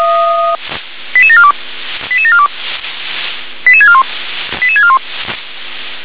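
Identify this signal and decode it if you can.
Five-tone selective call (ZVEI), heard as quick runs of five tones, tones of about 70 ms. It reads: 88742 88742 88641 88641
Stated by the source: ZVEI five-tone call 88712 received by SDR.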